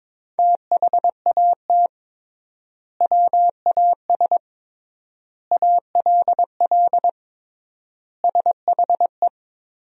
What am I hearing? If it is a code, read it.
Morse code, heard as a keyed tone, about 22 wpm, 700 Hz. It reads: THAT WAS ALL SHE
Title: 3x Word-Spacing - Sentences from Top 100 Words - 22wpm